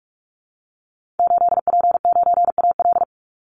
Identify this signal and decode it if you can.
Morse code, heard as a keyed tone, about 44 wpm, 700 Hz. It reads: 8P9AL
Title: Call Signs Repeated 2x - 45wpm